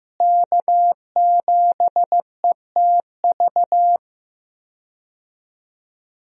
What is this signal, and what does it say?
Morse code, heard as a keyed tone, about 15 wpm, 700 Hz. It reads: K7ETV
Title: USA Call Signs - 5 Characters - Repeated 3x - 15wpm